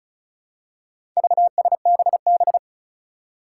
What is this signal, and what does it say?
Morse code, heard as a keyed tone, about 34 wpm, 700 Hz. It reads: VSBB
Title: Four Challenging Letter-Number Combinations 35wpm